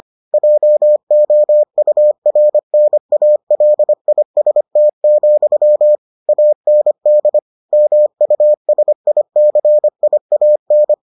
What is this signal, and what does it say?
Morse code, heard as a keyed tone, about 25 wpm, 600 Hz. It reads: JOURNALIST, AND MUSICIAN